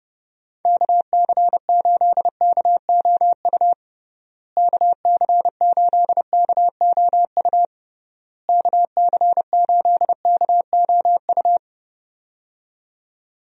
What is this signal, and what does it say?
Morse code, heard as a keyed tone, about 31 wpm, 700 Hz. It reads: KC8KOU KC8KOU KC8KOU
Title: USA Call Signs - 6 Characters - Repeated 3x - 30wpm